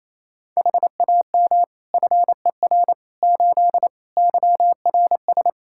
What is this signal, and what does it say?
Morse code, heard as a keyed tone, about 28 wpm, 700 Hz. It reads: HAM FER 8 YRS
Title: Practice QSO - Exchange 3 - More Info - 28wpm